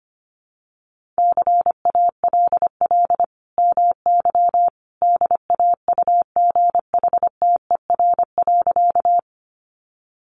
Morse code, heard as a keyed tone, about 25 wpm, 700 Hz. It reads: CALL MY DAUGHTER.
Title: Sentences from Top 1500 Words 25wpm